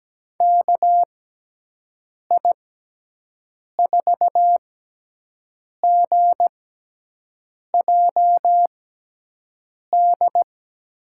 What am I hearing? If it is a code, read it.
Morse code, heard as a keyed tone, about 17 wpm, spaced as for 6 wpm, 700 Hz. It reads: KI4GJD
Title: USA Call Signs - 6 Characters - 6x Character Spacing - 17wpm